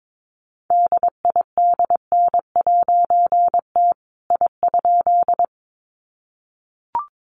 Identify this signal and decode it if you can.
Morse code, heard as a keyed tone, about 22 wpm, 700 Hz. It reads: DIDN'T I?
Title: A Sound Of Thunder 22wpm